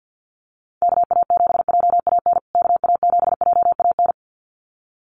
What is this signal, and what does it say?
Morse code, heard as a keyed tone, about 50 wpm, 700 Hz. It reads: KA7JAN KA7JAN
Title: USA Call Signs Repeated 2x 50wpm